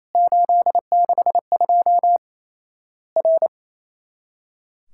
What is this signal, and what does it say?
Morse code, heard as a keyed tone, about 27 wpm, 700 Hz. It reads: 862 R
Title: Worked All Europe DX Contest - QTC Exchange - 28wpm